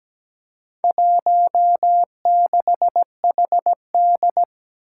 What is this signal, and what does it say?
Morse code, heard as a keyed tone, about 17 wpm, 700 Hz. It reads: 16HD
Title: Four Challenging Letter-Number Combinations 17wpm